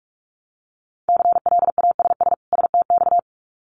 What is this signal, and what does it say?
Morse code, heard as a keyed tone, about 46 wpm, 700 Hz. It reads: CLASS STX
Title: Winter Field Day Exchanges 45wpm